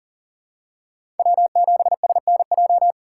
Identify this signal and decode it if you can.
Morse code, heard as a keyed tone, about 40 wpm, 700 Hz. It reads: W7SNJ
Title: USA Call Signs - 5 Characters - 2x Character Spacing - 40wpm